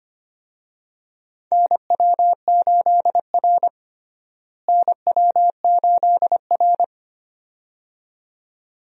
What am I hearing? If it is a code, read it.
Morse code, heard as a keyed tone, about 25 wpm, 700 Hz. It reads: NW8R NW8R
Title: USA Call Signs - 4 Characters - Repeated 2x - 25wpm